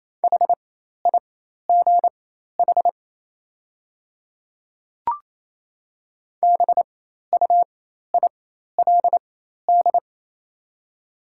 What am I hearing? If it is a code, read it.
Morse code, heard as a keyed tone, about 27 wpm, spaced as for 13 wpm, 690 Hz. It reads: HIGH BUILD
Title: 4x Character Spacing - Top 300 Words - 28wpm